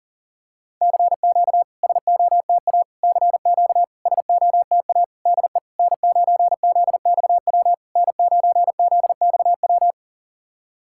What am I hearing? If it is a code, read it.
Morse code, heard as a keyed tone, about 37 wpm, 690 Hz. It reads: CQ SOTA CQ SOTA DE N9ZXW N9ZXW
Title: SOTA QSO Practice 40wpm